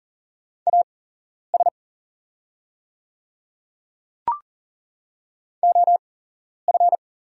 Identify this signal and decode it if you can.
Morse code, heard as a keyed tone, about 40 wpm, spaced as for 11 wpm, 700 Hz. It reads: AS OF
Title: 8x Character Spacing - Top 100 Words - 40wpm